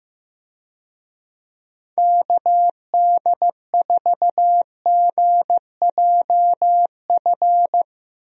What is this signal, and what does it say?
Morse code, heard as a keyed tone, about 15 wpm, 700 Hz. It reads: KD4GJF
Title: USA Call Signs - 6 Characters - Repeated 3x - 15wpm